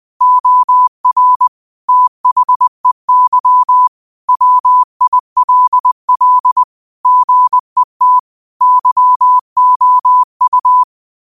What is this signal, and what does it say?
Morse code, heard as a keyed tone, about 20 wpm, 1000 Hz. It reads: OR THEY WILL GET YOU
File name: qotdfiles / 20wpm